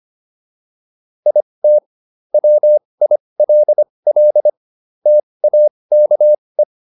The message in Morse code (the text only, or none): IT WILL TAKE